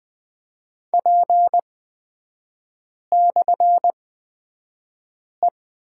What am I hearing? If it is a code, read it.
Morse code, heard as a keyed tone, about 20 wpm, spaced as for 5 wpm, 700 Hz. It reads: P/E
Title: 250 Lesson 40e - BK - Three Characters 20wpm